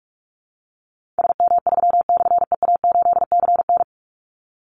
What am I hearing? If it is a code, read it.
Morse code, heard as a keyed tone, about 44 wpm, 700 Hz. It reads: SM3/EA8CN
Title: Call Signs Repeated 3x - 45wpm